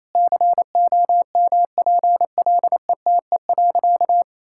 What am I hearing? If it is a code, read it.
Morse code, heard as a keyed tone, about 28 wpm, 700 Hz. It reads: COMPLETE.
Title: Sentences from Top 1000 Words - Review Part 6 - 28wpm